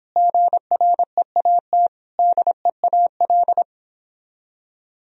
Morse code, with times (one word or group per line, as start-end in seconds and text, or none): 0.2-1.9 GREAT
2.2-3.6 DEAL